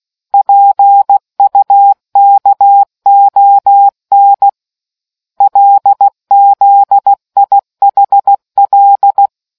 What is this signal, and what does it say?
Morse code, heard as a keyed tone, about 16 wpm, 800 Hz. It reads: PUKON LZIHL